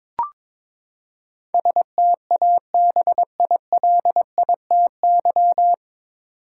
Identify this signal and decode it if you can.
Morse code, heard as a keyed tone, about 22 wpm, 700 Hz. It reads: STABILITY
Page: Morse Code Ninja practice 22wpm